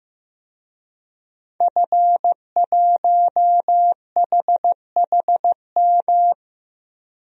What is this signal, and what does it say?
Morse code, heard as a keyed tone, about 15 wpm, 700 Hz. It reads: F1HHM
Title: DXCC-100 Call Signs - 5 Characters - Repeated 2x - 15wpm